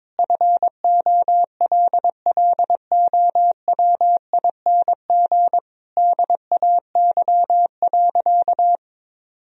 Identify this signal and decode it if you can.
Morse code, heard as a keyed tone, about 22 wpm, 700 Hz. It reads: FOLLOWING DAY.
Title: Sentences from Top 1000 Words - Review Part 4 - 22wpm